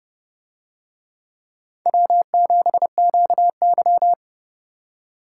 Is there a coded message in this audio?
Yes: Morse code from 1.9 to 4.1 s, heard as a keyed tone, about 29 wpm, 700 Hz. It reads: W7QY